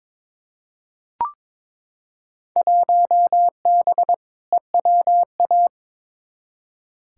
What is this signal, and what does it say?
Morse code, heard as a keyed tone, about 22 wpm, 700 Hz. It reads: 1B EWA
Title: Field Day Exchanges Encore - 22wpm